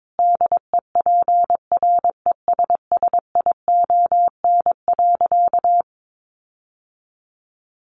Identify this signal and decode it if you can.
Morse code, heard as a keyed tone, about 22 wpm, 700 Hz. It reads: DEPRESSION.
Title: Sentences from Top 1300 Words 22wpm